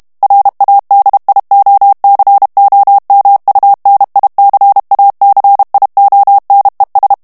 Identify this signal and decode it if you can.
Morse code, heard as a keyed tone, about 32 wpm, 800 Hz. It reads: RADIOCOMUNICACIONES
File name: ESPAGNOL TEXTE EN CLAIR 2 VITESSE 30 MOTS MINUTES